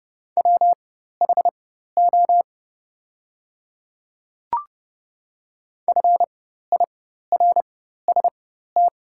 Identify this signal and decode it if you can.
Morse code, heard as a keyed tone, about 32 wpm, spaced as for 14 wpm, 700 Hz. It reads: WHO FIRST
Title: 4x Character Spacing - Top 100 Words - 30wpm